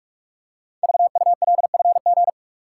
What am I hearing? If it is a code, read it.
Morse code, heard as a keyed tone, about 44 wpm, 700 Hz. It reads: VULFG
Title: Code Groups - 5 Letters - 45wpm